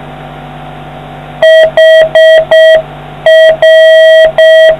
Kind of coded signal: Morse code